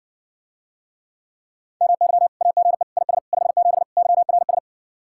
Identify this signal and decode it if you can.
Morse code, heard as a keyed tone, about 60 wpm, 700 Hz. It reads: MY AGE IS 57 YRS